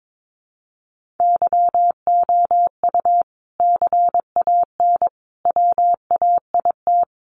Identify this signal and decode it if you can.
Morse code, heard as a keyed tone, about 22 wpm, 700 Hz. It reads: YOU CAN WAIT
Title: Sentences from Top 1400 Words 22wpm